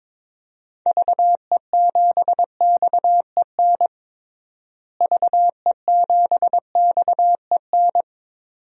Morse code, heard as a keyed tone, about 22 wpm, 700 Hz. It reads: VE7XEN VE7XEN
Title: DXCC-100 Call Signs - 6 Characters - Repeated 2x - 22wpm